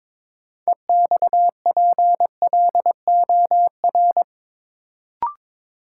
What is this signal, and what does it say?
Morse code, heard as a keyed tone, about 22 wpm, 700 Hz. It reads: EXPLOR
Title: Sentences from Top 1100 Words - New Word Drill Alt - 22wpm